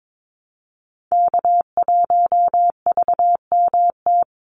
Morse code, heard as a keyed tone, about 22 wpm, 700 Hz. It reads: K1VMT
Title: Call Signs Repeated 2x - 22wpm